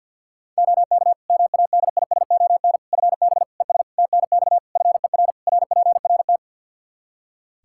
Morse code, heard as a keyed tone, about 48 wpm, 700 Hz. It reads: OK MADISON FB ES TNX FER RPRT
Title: Practice QSO - Exchange 2 - Info - 50wpm